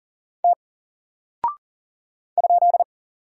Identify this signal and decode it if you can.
Morse code, heard as a keyed tone, about 40 wpm, 700 Hz. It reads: T ?
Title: Single Character (Letter:Number:Punctuation:Prosign) Alt 40wpm